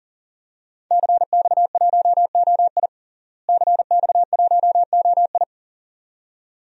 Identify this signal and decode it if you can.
Morse code, heard as a keyed tone, about 40 wpm, 700 Hz. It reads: CX1OI CX1OI
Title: DXCC-100 Call Signs - 5 Characters - Repeated 2x - 40wpm